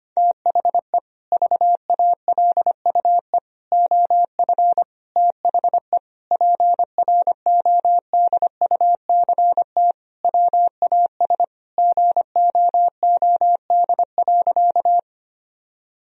Morse code, heard as a keyed tone, about 25 wpm, 700 Hz. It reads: THE VALUE OF THE PRODUCT WAS GOOD.
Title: Sentences from Top 1000 Words - Review Part 3 - 25wpm